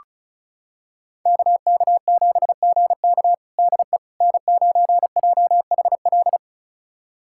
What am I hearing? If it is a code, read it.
Morse code, heard as a keyed tone, about 34 wpm, 700 Hz. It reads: KK7GK DE N9JHL
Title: USA Callsign DE Callsign - 5 Characters - 35wpm